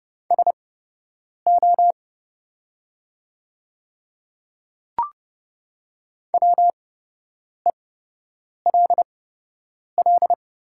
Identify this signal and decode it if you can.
Morse code, heard as a keyed tone, about 30 wpm, spaced as for 8 wpm, 700 Hz. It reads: SO WELL
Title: 8x Character Spacing - Top 200 Words - 30wpm